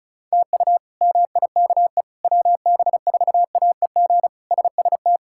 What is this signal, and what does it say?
Morse code, heard as a keyed tone, about 34 wpm, 700 Hz. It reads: TU MIKE WB4AEG SST
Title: K1USN SST QSO Practice - 35wpm